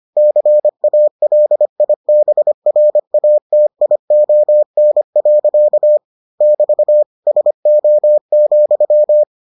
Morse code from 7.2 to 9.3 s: SO,

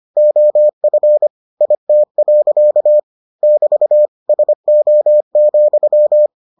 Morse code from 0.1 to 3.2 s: OF IT.

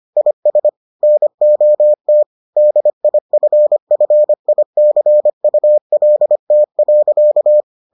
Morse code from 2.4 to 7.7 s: DIFFICULT.

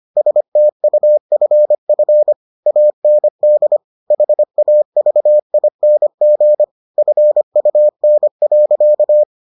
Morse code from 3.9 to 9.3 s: HAVING FUN.

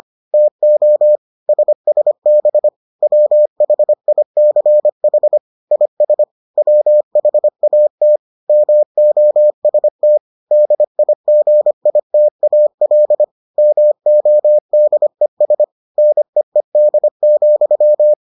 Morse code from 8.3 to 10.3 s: MOST

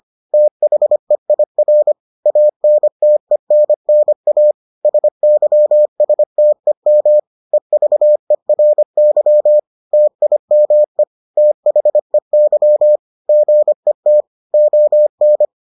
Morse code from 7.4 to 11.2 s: EVERY TIME

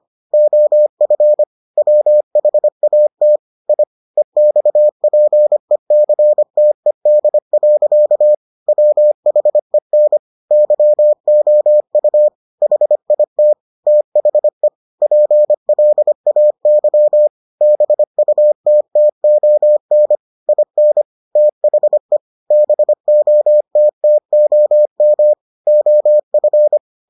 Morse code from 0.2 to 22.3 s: OF WHAT I EXPECTED. WHEN YOU HIT THE PLAY BUTTON IN THE

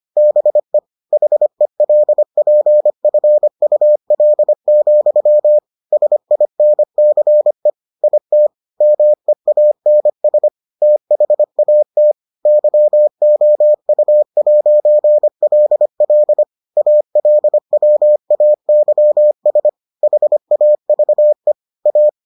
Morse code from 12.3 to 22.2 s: YOU'LL ALWAYS HAVE A

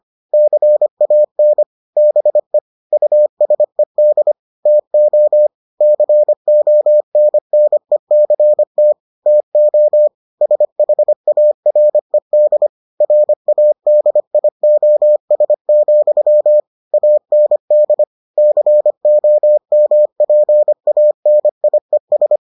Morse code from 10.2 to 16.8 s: SHARED RADIOS,

